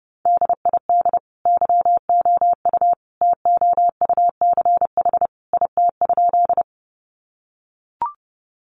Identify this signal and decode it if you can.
Morse code, heard as a keyed tone, about 30 wpm, 700 Hz. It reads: DID YOU TOUCH IT?
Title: Sentences from Top 1200 Words 30wpm